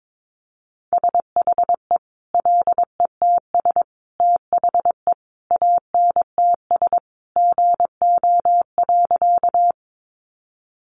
Morse code, heard as a keyed tone, about 22 wpm, 700 Hz. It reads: SHE LETS THE ANTS GO.
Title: Additional Easy Phrases 22wpm